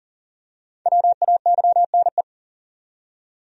Morse code, heard as a keyed tone, about 37 wpm, 700 Hz. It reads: WAYNE
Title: USA Common Names - 40wpm